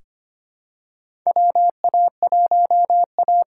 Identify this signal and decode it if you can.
Morse code, heard as a keyed tone, about 25 wpm, 700 Hz. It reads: WA1A
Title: USA Call Signs - 4 Characters - 2x Character Spacing - 25wpm